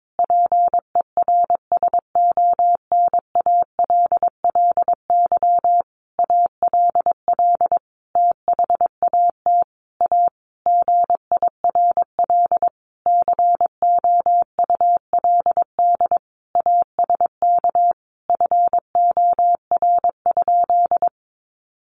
Morse code, heard as a keyed tone, about 22 wpm, 700 Hz. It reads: PERSONALLY ALL THAT A GIRL COULD ASK FOR?
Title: Sentences from Top 1400 Words 22wpm